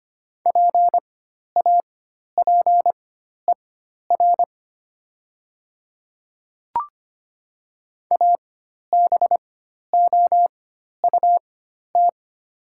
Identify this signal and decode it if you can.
Morse code, heard as a keyed tone, about 25 wpm, spaced as for 12 wpm, 700 Hz. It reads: PAPER ABOUT
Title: 4x Character Spacing - Top 300 Words - 25wpm